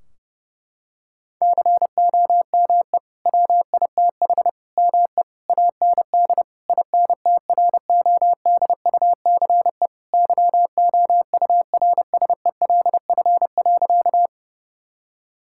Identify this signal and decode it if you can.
Morse code, heard as a keyed tone, about 30 wpm, 700 Hz. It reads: COME WITH ME AND INTRODUCE YOURSELF.
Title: Sentences from Top 1600 Words 30wpm